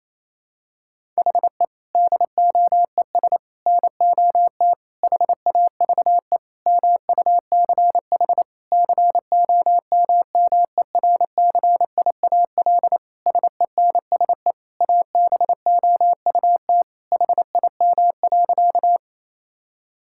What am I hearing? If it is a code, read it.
Morse code, heard as a keyed tone, about 28 wpm, 700 Hz. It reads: HE DOES NOT HAVE MUCH COMMERCIAL SENSE ABOUT HIM.
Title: Sentences from Top 1000 Words - Review Part 5 - 28wpm